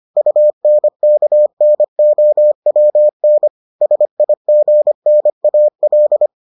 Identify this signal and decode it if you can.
Morse code, heard as a keyed tone, about 25 wpm, 600 Hz. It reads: UNKNOWN SIGNAL